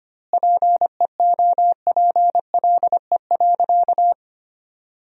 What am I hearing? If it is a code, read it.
Morse code, heard as a keyed tone, about 25 wpm, 700 Hz. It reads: PEOPLE.
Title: 2x Word-Spacing - Sentences from Top 500 Words - 25wpm